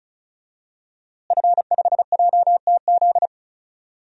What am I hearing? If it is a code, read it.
Morse code, heard as a keyed tone, about 34 wpm, 700 Hz. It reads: F5JTZ
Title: DXCC-100 Call Signs - 5 Characters - Repeated 2x - 35wpm